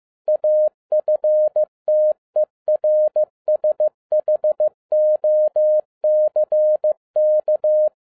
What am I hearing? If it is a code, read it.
Morse code, heard as a keyed tone, about 15 wpm, 610 Hz. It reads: AFTERSHOCK
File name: 2011-10-mp3 / n02-FIELD DAY CONTEST QRV FROM SAKAE VILLAGE 15wpm